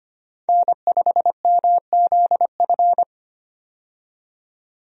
Morse code, heard as a keyed tone, about 25 wpm, 700 Hz. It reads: N5MZF
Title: USA Call Signs - 5 Characters - 25wpm